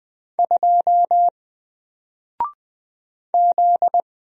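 Morse code, heard as a keyed tone, about 20 wpm, 700 Hz. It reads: 2 Z
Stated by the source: Single letter and number Morse drill at 20 wpm, from N4LCD.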